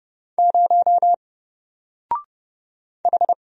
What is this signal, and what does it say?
Morse code, heard as a keyed tone, about 29 wpm, 700 Hz. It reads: 0 H
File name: Single-Letter-Number-30wpm